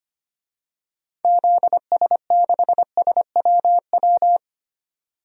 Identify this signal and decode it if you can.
Morse code, heard as a keyed tone, about 25 wpm, 700 Hz. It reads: ZS6SWW
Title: DXCC-100 Call Signs - 6 Characters - 25wpm